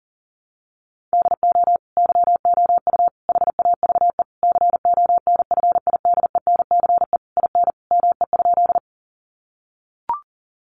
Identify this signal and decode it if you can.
Morse code, heard as a keyed tone, about 40 wpm, 700 Hz. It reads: DO YOU HAVE CONFIDENCE IN ME?
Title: Sentences from Top 1500 Words 40wpm